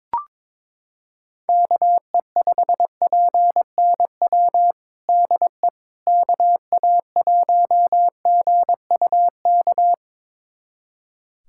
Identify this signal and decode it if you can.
Morse code, heard as a keyed tone, about 22 wpm, 700 Hz. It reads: KE5PNW DE KA1GUK
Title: USA Callsign DE Callsign - 6 Characters - 22wpm